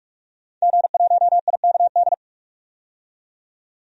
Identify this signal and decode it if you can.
Morse code, heard as a keyed tone, about 46 wpm, 700 Hz. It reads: G1IKD
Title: DXCC-100 Call Signs - 5 Characters - 45wpm